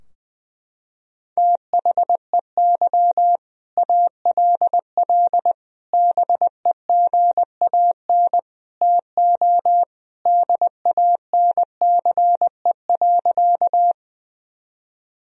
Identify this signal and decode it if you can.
Morse code, heard as a keyed tone, about 20 wpm, 700 Hz. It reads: THEY ALL BEGAN TO DANCE.
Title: Sentences from Top 1000 Words - Review Part 4 - 20wpm